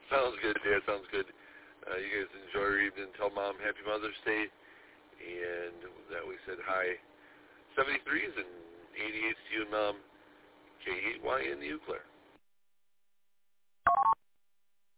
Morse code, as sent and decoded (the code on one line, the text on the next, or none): none